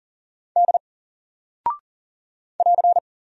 Morse code, heard as a keyed tone, about 40 wpm, 700 Hz. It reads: D <AR>
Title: Single Character (Letter:Number:Punctuation:Prosign) 40wpm